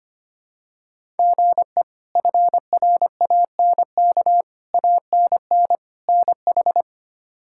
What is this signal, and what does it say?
Morse code, heard as a keyed tone, about 25 wpm, 700 Hz. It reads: GE FRANK ANN NH
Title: K1USN SST QSO Practice - 25wpm